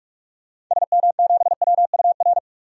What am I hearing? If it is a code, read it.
Morse code, heard as a keyed tone, about 41 wpm, 700 Hz. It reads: SM7WUR
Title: DXCC-100 Call Signs - 6 Characters - 45wpm